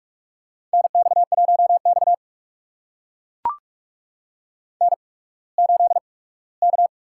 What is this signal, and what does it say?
Morse code, heard as a keyed tone, about 44 wpm, spaced as for 21 wpm, 690 Hz. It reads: NX1X N8K
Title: USA Call Signs - 4 Characters - 8x Character Spacing - 45wpm